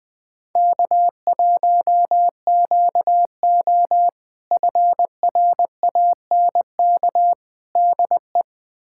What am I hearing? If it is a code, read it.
Morse code, heard as a keyed tone, about 21 wpm, 700 Hz. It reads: K1QO FRANK DE
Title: K1USN SST QSO Practice - 20wpm